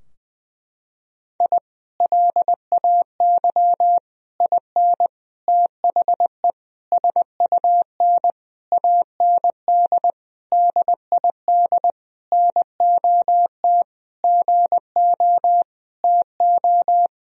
Morse code, heard as a keyed tone, about 19 wpm, 700 Hz. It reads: I LAY IN THE SUN AND DID NOT GO TO
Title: Sentences from Top 1000 Words - Review Part 2 - 20wpm